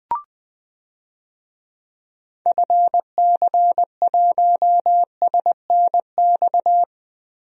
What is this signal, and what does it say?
Morse code, heard as a keyed tone, about 20 wpm, 700 Hz. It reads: FC1SNX
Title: DXCC-100 Call Signs - 6 Characters - 20wpm